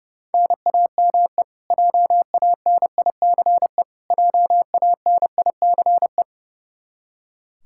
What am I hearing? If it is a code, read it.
Morse code, heard as a keyed tone, about 30 wpm, 700 Hz. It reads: NAME JANICE JANICE
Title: Practice QSO - Exchange 1 - Beginning - 30wpm